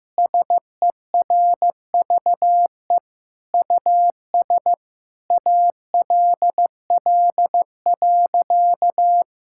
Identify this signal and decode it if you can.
Morse code, heard as a keyed tone, about 15 wpm, 690 Hz. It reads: SERVE US ALL.
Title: Sentences from Top 1000 Words - Review Part 5 - 15wpm